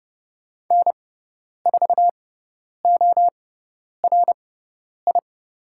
Morse code, heard as a keyed tone, about 30 wpm, spaced as for 10 wpm, 700 Hz. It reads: N4ORI
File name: Lesson 16g - Letter c - Call Signs 30wpm